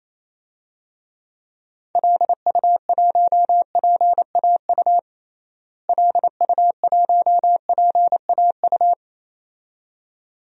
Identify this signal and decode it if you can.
Morse code, heard as a keyed tone, about 28 wpm, 700 Hz. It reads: LU1PAU LU1PAU
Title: DXCC-100 Call Signs - 6 Characters - Repeated 2x - 28wpm